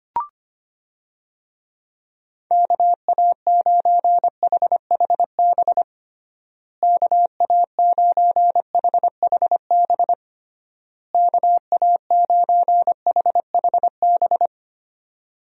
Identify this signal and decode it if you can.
Morse code, heard as a keyed tone, about 25 wpm, 700 Hz. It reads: KA9HHB KA9HHB KA9HHB
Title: USA Call Signs - 6 Characters - Repeated 3x - 25wpm